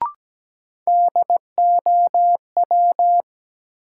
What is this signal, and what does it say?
Morse code, heard as a keyed tone, about 17 wpm, 700 Hz. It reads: DOW
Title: Sets of 5 Words - New Word Drill - 17wpm